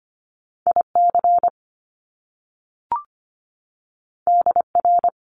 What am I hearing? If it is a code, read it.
Morse code, heard as a keyed tone, about 25 wpm, 700 Hz. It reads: IC DR
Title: Top Two Letter Combinations in Words 25wpm